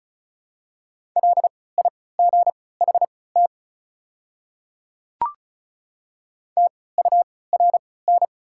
Morse code, heard as a keyed tone, about 36 wpm, spaced as for 20 wpm, 690 Hz. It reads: LIGHT TURN
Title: 3x Character Spacing - Top 200 Words - 35wpm